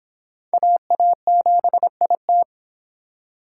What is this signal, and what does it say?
Morse code, heard as a keyed tone, about 21 wpm, 700 Hz. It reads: AA7IT
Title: Speed Racing - USA 2x2 Call Signs 17wpm